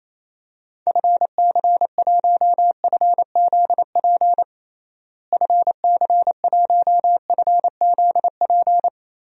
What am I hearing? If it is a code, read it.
Morse code, heard as a keyed tone, about 28 wpm, 700 Hz. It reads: FC1FZP FC1FZP
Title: DXCC-100 Call Signs - 6 Characters - Repeated 3x - 28wpm